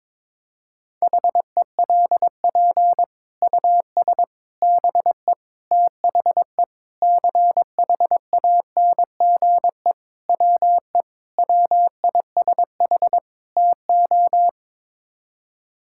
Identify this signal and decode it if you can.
Morse code, heard as a keyed tone, about 22 wpm, 700 Hz. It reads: HELP US BE THE CHANGE WE WISH TO
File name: practice / 22wpm / Sets of 8 Words - 22wpm